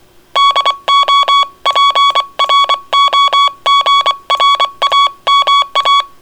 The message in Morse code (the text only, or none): DO PROGRAMA